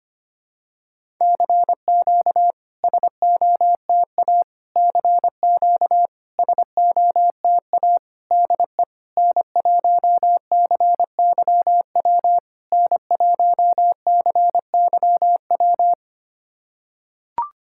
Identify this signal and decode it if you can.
Morse code, heard as a keyed tone, about 25 wpm, 700 Hz. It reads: CQ SOTA CQ SOTA DE N1CYW N1CYW
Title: SOTA QSO Practice 25wpm